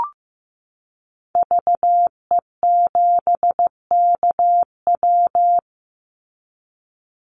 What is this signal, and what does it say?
Morse code, heard as a keyed tone, about 15 wpm, 700 Hz. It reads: VE7KW
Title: Call Signs Repeated 2x - 15wpm